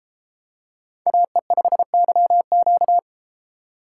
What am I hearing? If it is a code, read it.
Morse code, heard as a keyed tone, about 32 wpm, 700 Hz. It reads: AE5YQ